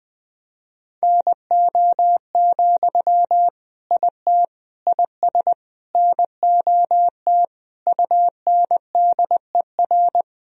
Morse code, heard as a keyed tone, about 20 wpm, 700 Hz. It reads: NO, IT IS NOT UNDER
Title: Sentences from Top 1000 Words - Review Part 1 - 20wpm